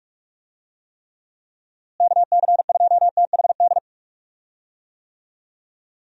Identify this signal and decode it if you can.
Morse code, heard as a keyed tone, about 46 wpm, 700 Hz. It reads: KC2THD
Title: USA Call Signs - 6 Characters - Repeated 3x - 45wpm